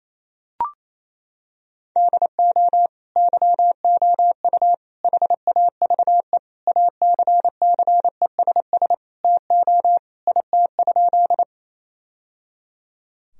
Morse code, heard as a keyed tone, about 28 wpm, 700 Hz. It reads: DO YOU HAVE ACCESS TO IT?
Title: Sentences from Top 1000 Words - Review Part 4 - 28wpm